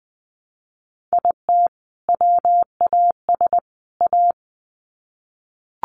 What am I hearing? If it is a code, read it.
Morse code, heard as a keyed tone, about 20 wpm, 700 Hz. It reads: IT WAS A
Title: Sets of 3 Words 20wpm